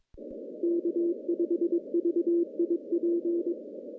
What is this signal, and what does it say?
Morse code, heard as a keyed tone, about 22 wpm, 340 Hz. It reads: K5VIP
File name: contests / CW-SS-k5tr- -1400